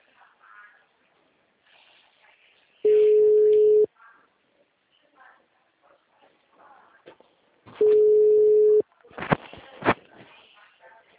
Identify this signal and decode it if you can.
Ringback tone (Europe, 425 Hz), with speech.